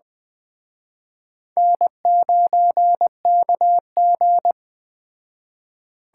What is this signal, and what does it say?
Morse code, heard as a keyed tone, about 21 wpm, 700 Hz. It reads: N9KG
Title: Speed Racing - USA Call Signs 17wpm